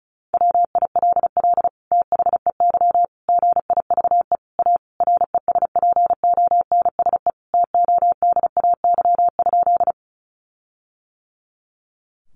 Morse code, heard as a keyed tone, about 34 wpm, 700 Hz. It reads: WILL THEY GIVE A RESPONSE TODAY?
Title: Sentences from Top 1400 Words 35wpm